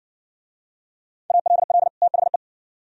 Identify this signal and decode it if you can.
Morse code, heard as a keyed tone, about 58 wpm, 700 Hz. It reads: ALL THE